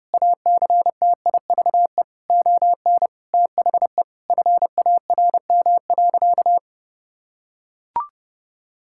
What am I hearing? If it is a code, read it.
Morse code, heard as a keyed tone, about 32 wpm, 700 Hz. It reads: ACTIVE ON THE FARM.
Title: Sentences from Top 1000 Words - Review Part 6 - 30wpm